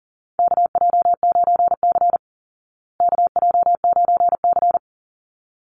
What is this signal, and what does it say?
Morse code, heard as a keyed tone, about 40 wpm, 700 Hz. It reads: KJ9C KJ9C
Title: Call Signs Repeated 3x - 40wpm